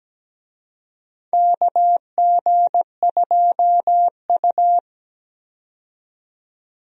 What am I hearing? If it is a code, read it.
Morse code, heard as a keyed tone, about 17 wpm, 700 Hz. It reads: KG2U
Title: USA Call Signs - 4 Characters - 17wpm